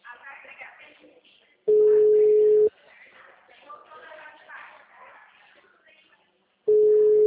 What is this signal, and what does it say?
Ringback tone (Europe, 425 Hz), with speech.
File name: out- -2004- - - .6